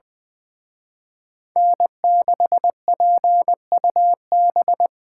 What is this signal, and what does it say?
Morse code, heard as a keyed tone, about 24 wpm, 700 Hz. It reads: N6PUB
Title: Speed Racing - USA 1x3 Call Signs 20wpm